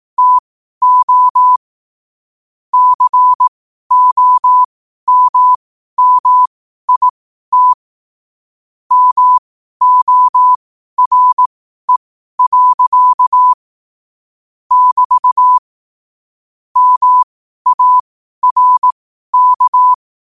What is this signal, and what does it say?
Morse code, heard as a keyed tone, about 18 wpm, spaced as for 12 wpm, 1000 Hz. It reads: TO COMMIT MORE. <BT> MARK
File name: qotdfiles / 12wpm